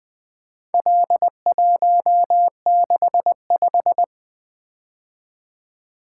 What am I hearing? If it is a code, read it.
Morse code, heard as a keyed tone, about 20 wpm, 700 Hz. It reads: L165
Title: Four Letter-Number Combinations - 20wpm